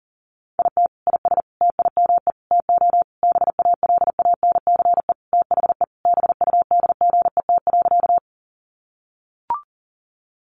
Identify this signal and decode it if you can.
Morse code, heard as a keyed tone, about 40 wpm, 700 Hz. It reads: IT IS TIME TO BALANCE THE BUDGET.
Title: Sentences from Top 1200 Words 40wpm